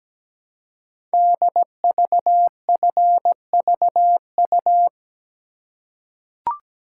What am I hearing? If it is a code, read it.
Morse code, heard as a keyed tone, about 17 wpm, 700 Hz. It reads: DVFVU
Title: Code Groups - 5 Letters - 17wpm